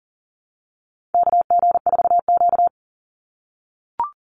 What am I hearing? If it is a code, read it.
Morse code, heard as a keyed tone, about 40 wpm, 700 Hz. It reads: KG4Q
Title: Call Signs - 40wpm